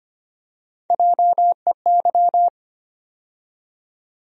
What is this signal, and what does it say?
Morse code, heard as a keyed tone, about 25 wpm, 700 Hz. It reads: JEY
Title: USA Call Sign Postfixes - 3 Letters 25wpm